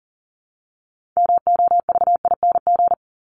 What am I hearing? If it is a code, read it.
Morse code, heard as a keyed tone, about 40 wpm, 700 Hz. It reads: MOVING
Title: ING Suffix - Top 100 Words with Base Words 40wpm